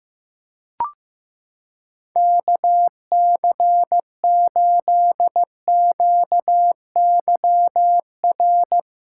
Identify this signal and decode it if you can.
Morse code, heard as a keyed tone, about 15 wpm, 700 Hz. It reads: KC8QYR
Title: USA Callsign DE Callsign - 6 Characters - 15wpm